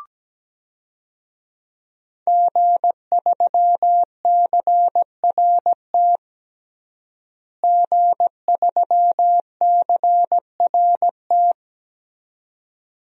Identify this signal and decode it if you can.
Morse code, heard as a keyed tone, about 17 wpm, 700 Hz. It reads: G3CRT G3CRT
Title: DXCC-100 Call Signs - 5 Characters - Repeated 2x - 17wpm